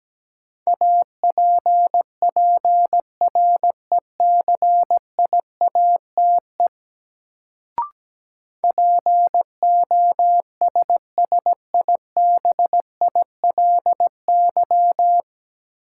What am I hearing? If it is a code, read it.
Morse code, heard as a keyed tone, about 17 wpm, 700 Hz. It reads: APPRECIATE POSSIBILY
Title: Sentences from Top 1300 Words - New Word Drill - 17wpm